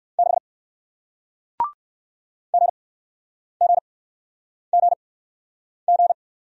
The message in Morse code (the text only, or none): 6 GGGG